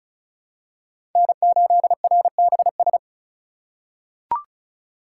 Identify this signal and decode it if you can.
Morse code, heard as a keyed tone, about 36 wpm, 700 Hz. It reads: N8RBS
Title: USA Call Signs - 5 Characters - 2x Character Spacing - 35wpm